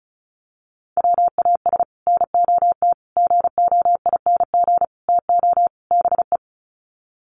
Morse code, heard as a keyed tone, about 34 wpm, 700 Hz. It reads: WAS NOT GOING TO BE
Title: Sets of 5 Words Encore 35wpm